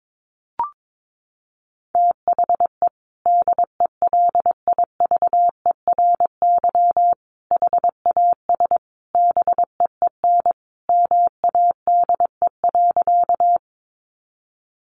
Morse code, heard as a keyed tone, about 22 wpm, 700 Hz. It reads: THE DELIVERY HAS BEEN MADE.
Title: Sentences from Top 1500 Words 22wpm